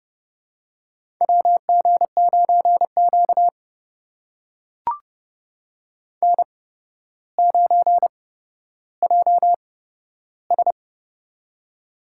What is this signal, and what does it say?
Morse code, heard as a keyed tone, about 30 wpm, spaced as for 13 wpm, 700 Hz. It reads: WG9Q N9JS
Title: USA Call Signs - 4 Characters - 8x Character Spacing - 30wpm